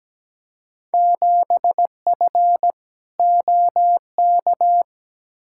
Morse code, heard as a keyed tone, about 17 wpm, 700 Hz. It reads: 7F OK
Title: Field Day Exchanges Encore - 17wpm